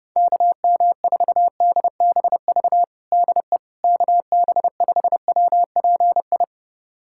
Morse code, heard as a keyed tone, about 30 wpm, 700 Hz. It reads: KM4DBV DE KB5WPI
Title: USA Callsign DE Callsign - 6 Characters - 30wpm